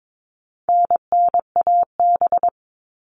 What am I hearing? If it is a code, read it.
Morse code, heard as a keyed tone, about 22 wpm, 700 Hz. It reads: NNAB
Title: All Cut Numbers - 4-Digit 22wpm